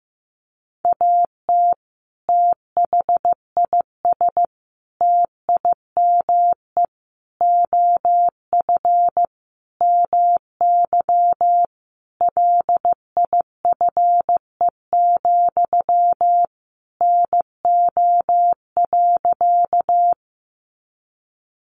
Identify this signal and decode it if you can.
Morse code, heard as a keyed tone, about 15 wpm, 700 Hz. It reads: AT THIS TIME OF MY LIFE, NO.
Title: Sentences from Top 300 Words 15wpm